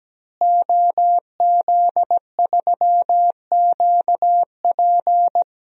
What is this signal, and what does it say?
Morse code, heard as a keyed tone, about 17 wpm, 700 Hz. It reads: OZ3QP
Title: DXCC-100 Call Signs - 5 Characters - Repeated 3x - 17wpm